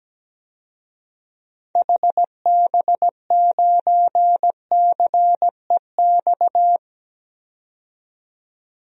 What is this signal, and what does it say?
Morse code, heard as a keyed tone, about 17 wpm, 700 Hz. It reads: HB9CEX
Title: DXCC-100 Call Signs - 6 Characters - 17wpm